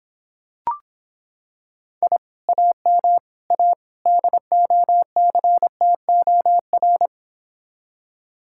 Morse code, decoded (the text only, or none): I AM A DOCTOR